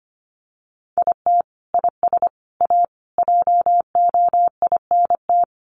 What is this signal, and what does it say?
Morse code, heard as a keyed tone, about 25 wpm, 700 Hz. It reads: IT IS A JOINT
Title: Sentences from Top 1300 Words 25wpm